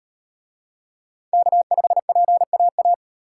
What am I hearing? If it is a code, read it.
Morse code, heard as a keyed tone, about 35 wpm, 700 Hz. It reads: K5PAA